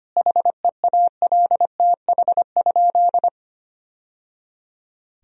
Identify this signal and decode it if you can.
Morse code, heard as a keyed tone, about 25 wpm, 700 Hz. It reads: HEALTH?
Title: Sentences from Top 1000 Words - Review Part 3 - 25wpm